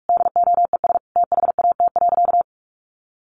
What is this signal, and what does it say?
Morse code, heard as a keyed tone, about 44 wpm, 700 Hz. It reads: DOES THAT.
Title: Sentences from Top 1400 Words 45wpm